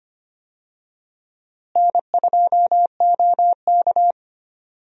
Morse code, heard as a keyed tone, about 25 wpm, 700 Hz. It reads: N2OK
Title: USA Call Signs - 4 Characters - Repeated 3x - 25wpm